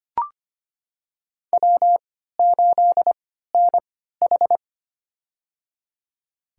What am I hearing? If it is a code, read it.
Morse code, heard as a keyed tone, about 25 wpm, spaced as for 14 wpm, 700 Hz. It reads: W8NH
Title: USA Call Signs - 4 Characters - 3x Character Spacing - 25wpm